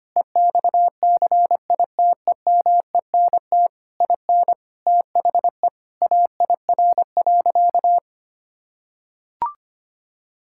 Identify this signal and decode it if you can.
Morse code, heard as a keyed tone, about 25 wpm, 700 Hz. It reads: EXCITEMENT IN THE AIR.
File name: practice / 25wpm / Sentences with 3-Word Phrases - 25wpm